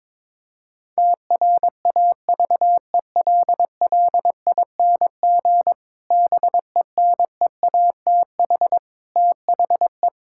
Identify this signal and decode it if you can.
Morse code, heard as a keyed tone, about 22 wpm, 700 Hz. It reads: TRAVELLING BENEATH THE
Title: Pangram Challenge v2 - 22wpm